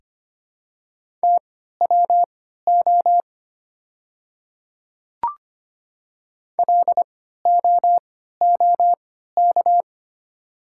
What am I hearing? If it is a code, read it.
Morse code, heard as a keyed tone, about 25 wpm, spaced as for 14 wpm, 700 Hz. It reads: TWO LOOK
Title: 3x Character Spacing - Top 200 Words - 25wpm